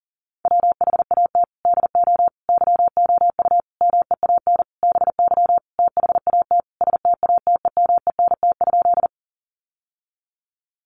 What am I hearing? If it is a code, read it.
Morse code, heard as a keyed tone, about 40 wpm, 700 Hz. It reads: WHAT DO YOU MEAN BY THAT STATEMENT?
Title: Sentences from Top 1200 Words 40wpm